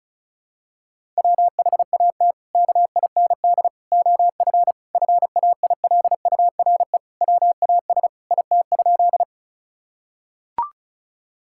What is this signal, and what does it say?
Morse code, heard as a keyed tone, about 34 wpm, 700 Hz. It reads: WHAT KIND OF FAILURE WAS IT?
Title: Sentences from Top 1000 Words - Review Part 6 - 35wpm